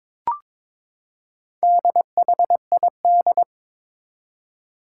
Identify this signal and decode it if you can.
Morse code, heard as a keyed tone, about 22 wpm, 700 Hz. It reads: DHID
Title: Four Challenging Letter-Number Combinations 22wpm